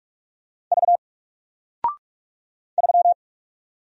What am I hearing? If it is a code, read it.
Morse code, heard as a keyed tone, about 46 wpm, 700 Hz. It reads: V 3
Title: Single Character (Letter:Number:Punctuation:Prosign) Alt 45wpm